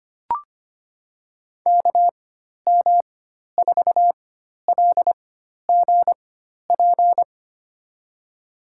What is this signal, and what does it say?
Morse code, heard as a keyed tone, about 25 wpm, spaced as for 12 wpm, 700 Hz. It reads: KM4LGP
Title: USA Call Signs - 6 Characters - 4x Character Spacing - 25wpm